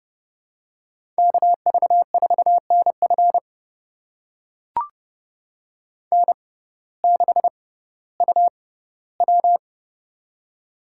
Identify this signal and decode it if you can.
Morse code, heard as a keyed tone, about 30 wpm, spaced as for 16 wpm, 700 Hz. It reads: KV4NF N6UW
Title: USA Call Signs - 6x Character Spacing - 30wpm